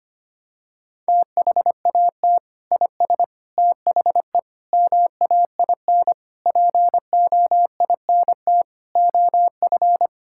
Morse code, heard as a keyed tone, about 25 wpm, 700 Hz. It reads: THAT IS THE MAIN POINT OF
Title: Sentences from Top 1000 Words - Review Part 5 - 25wpm